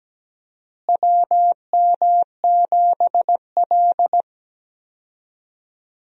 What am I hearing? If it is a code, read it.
Morse code, heard as a keyed tone, about 17 wpm, 700 Hz. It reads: WM7L
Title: USA Call Signs - 4 Characters - Repeated 3x - 17wpm